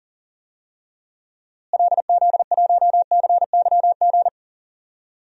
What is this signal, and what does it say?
Morse code, heard as a keyed tone, about 40 wpm, 700 Hz. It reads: LZ1CYG
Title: DXCC-100 Call Signs - 6 Characters - Repeated 2x - 40wpm